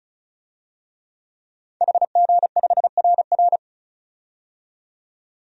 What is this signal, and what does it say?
Morse code, heard as a keyed tone, about 34 wpm, 700 Hz. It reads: HG5RR
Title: DXCC-100 Call Signs - 5 Characters - 35wpm